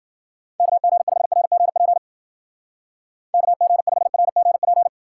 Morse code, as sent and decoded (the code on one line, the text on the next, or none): -.- --. ..... .-. --. .--. / -.- --. ..... .-. --. .--.
KG5RGP KG5RGP